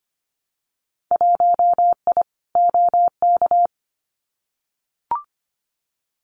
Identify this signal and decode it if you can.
Morse code, heard as a keyed tone, about 25 wpm, 700 Hz. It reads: 1I OK
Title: Winter Field Day Exchanges Encore 25wpm